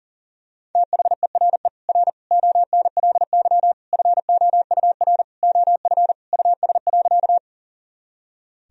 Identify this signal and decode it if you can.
Morse code, heard as a keyed tone, about 40 wpm, 700 Hz. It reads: THERE R ONLY FOUR OF US.
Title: Sentences from Top 300 Words and Abbreviations 40wpm